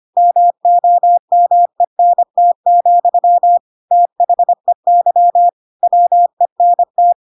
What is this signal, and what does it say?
Morse code, heard as a keyed tone, about 25 wpm, 700 Hz. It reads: MOMENT, THEY WENT
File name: Pinocchio / Pinocchio ch2 25 25 ik1hge